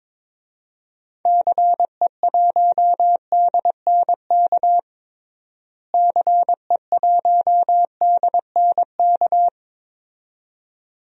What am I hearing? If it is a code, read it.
Morse code, heard as a keyed tone, about 22 wpm, 700 Hz. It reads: CE1DNK CE1DNK
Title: DXCC-100 Call Signs - 6 Characters - Repeated 2x - 22wpm